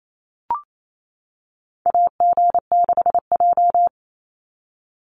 Morse code, heard as a keyed tone, about 28 wpm, 700 Hz. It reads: AG6J
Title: USA Call Signs 28wpm